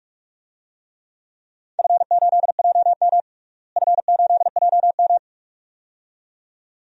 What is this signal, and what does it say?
Morse code, heard as a keyed tone, about 46 wpm, 700 Hz. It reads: F8JM F8JM
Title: DXCC-100 Call Signs - 4 Characters - Repeated 2x - 45wpm